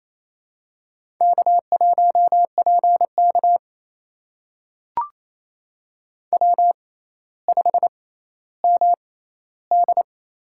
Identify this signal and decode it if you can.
Morse code, heard as a keyed tone, about 28 wpm, spaced as for 14 wpm, 700 Hz. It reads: K1PK W5MD
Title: USA Call Signs - 4 Characters - 6x Character Spacing - 28wpm